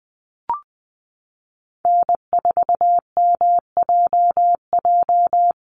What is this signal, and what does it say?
Morse code, heard as a keyed tone, about 20 wpm, 700 Hz. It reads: N4MJJ
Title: USA Call Signs Repeated 3x 20wpm